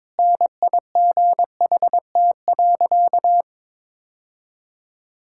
Morse code, heard as a keyed tone, about 22 wpm, 700 Hz. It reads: NIGHT.
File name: practice / 22wpm / Sentences with 2-Word Phrases - 22wpm